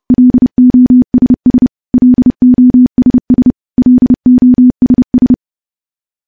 Morse code, heard as a keyed tone, about 30 wpm, 250 Hz. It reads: LOSS LOSS LOSS
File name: morse / morse-training-30WPM